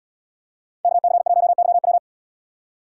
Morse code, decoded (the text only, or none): KC1YW